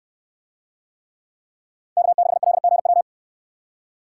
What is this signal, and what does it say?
Morse code, heard as a keyed tone, about 68 wpm, 700 Hz. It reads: K6LGW